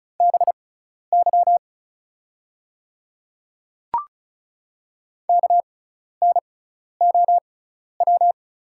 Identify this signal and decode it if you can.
Morse code, heard as a keyed tone, about 36 wpm, spaced as for 12 wpm, 700 Hz. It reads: BY KNOW
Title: 6x Character Spacing - Top 100 Words - 35wpm